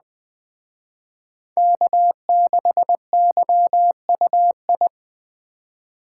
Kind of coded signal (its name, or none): Morse code